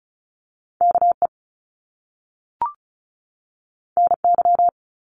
Morse code, heard as a keyed tone, about 36 wpm, 700 Hz. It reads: KE NY
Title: Top Two Letter Combinations in Words 35wpm